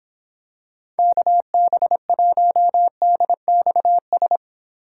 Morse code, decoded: KB1DXS

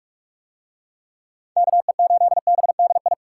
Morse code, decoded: KE8BDI

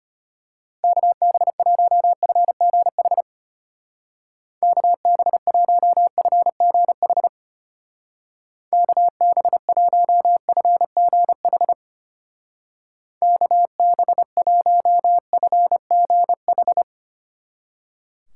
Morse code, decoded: KB1FGH KB1FGH KB1FGH KB1FGH

